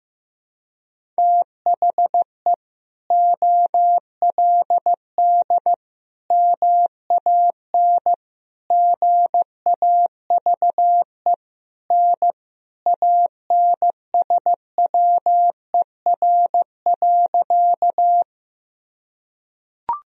Morse code, heard as a keyed tone, about 15 wpm, 700 Hz. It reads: THE OLD MAN GAVE N ANSWER.
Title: Sentences from Top 500 Words and Abbreviations 15wpm